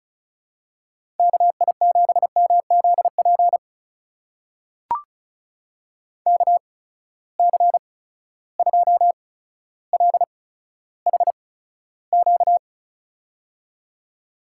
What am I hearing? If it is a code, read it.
Morse code, heard as a keyed tone, about 36 wpm, spaced as for 15 wpm, 700 Hz. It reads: KI7MZP KC2LHQ
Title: USA Call Signs - 6 Characters - 8x Character Spacing - 35wpm